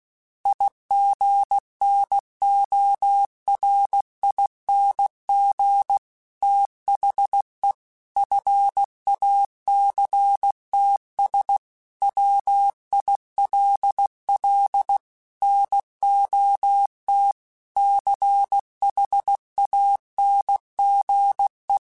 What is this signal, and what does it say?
Morse code, heard as a keyed tone, about 16 wpm, 800 Hz. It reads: IGNORING THE FACTS WILL NOT CHANGE
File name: Quotes / Quotes-mp3s / IgnoringFacts @15